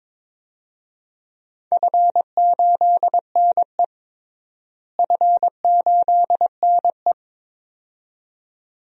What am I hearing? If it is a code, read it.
Morse code, heard as a keyed tone, about 22 wpm, 700 Hz. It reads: F8NE F8NE
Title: DXCC-100 Call Signs - 4 Characters - Repeated 2x - 22wpm